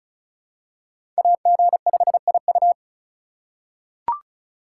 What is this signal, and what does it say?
Morse code, heard as a keyed tone, about 34 wpm, 700 Hz. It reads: AG5IU
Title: USA Call Signs - 5 Characters - 4x Character Spacing - 35wpm